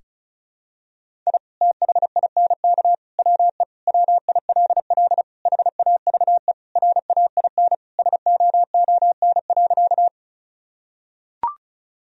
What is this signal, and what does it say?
Morse code, heard as a keyed tone, about 36 wpm, 700 Hz. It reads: I THINK WE WILL HAVE RAIN SOON.
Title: Sentences from Top 1000 Words - Review Part 1 - 35wpm